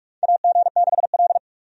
Morse code, heard as a keyed tone, about 46 wpm, 700 Hz. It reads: AG6L
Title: USA Call Signs - 4 Characters - 45wpm